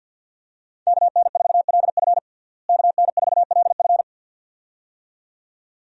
Morse code, heard as a keyed tone, about 48 wpm, 700 Hz. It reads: KN4LF KN4LF
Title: USA Call Signs - 5 Characters - Repeated 2x - 50wpm